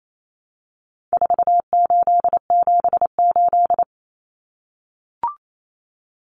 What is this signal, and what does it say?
Morse code, heard as a keyed tone, about 28 wpm, 700 Hz. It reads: 4878
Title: Common Cut Numbers - 4-Digit 28wpm